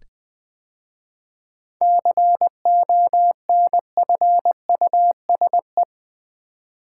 Morse code, heard as a keyed tone, about 20 wpm, 700 Hz. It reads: CONFUSE
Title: 7 Letter Words - First 3 Letters Given - 20wpm